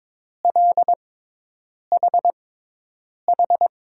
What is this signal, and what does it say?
Morse code, heard as a keyed tone, about 22 wpm, spaced as for 8 wpm, 700 Hz. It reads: LHH
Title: USA Call Signs - 6 Characters - 6x Character Spacing - 22wpm